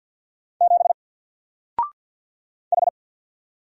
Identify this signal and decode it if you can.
Morse code, heard as a keyed tone, about 50 wpm, 700 Hz. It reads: 7 H
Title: Single Character (Letter:Number:Punctuation:Prosign) 50wpm